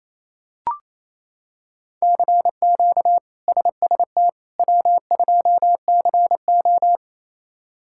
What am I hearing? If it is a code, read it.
Morse code, heard as a keyed tone, about 28 wpm, 700 Hz. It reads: CQ SST W2CO
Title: K1USN SST QSO Practice - 28wpm